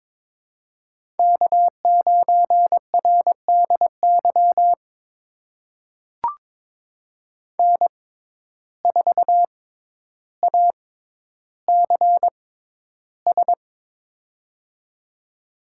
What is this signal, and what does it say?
Morse code, heard as a keyed tone, about 22 wpm, spaced as for 11 wpm, 700 Hz. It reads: K9RDY N4ACS
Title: USA Call Signs - 5 Characters - 6x Character Spacing - 22wpm